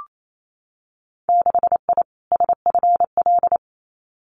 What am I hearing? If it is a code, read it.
Morse code, heard as a keyed tone, about 28 wpm, 700 Hz. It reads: 6I SFL
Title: Winter Field Day Exchanges 28wpm